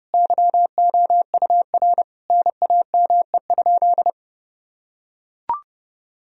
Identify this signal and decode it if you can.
Morse code, heard as a keyed tone, about 32 wpm, 700 Hz. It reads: YOUR NAME?
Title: Sentences from Top 1000 Words - Review Part 3 - 30wpm